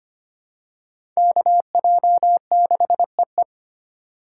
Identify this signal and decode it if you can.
Morse code, heard as a keyed tone, about 25 wpm, 700 Hz. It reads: KJ6EE
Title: USA Call Signs - 2x Character Spacing - 25wpm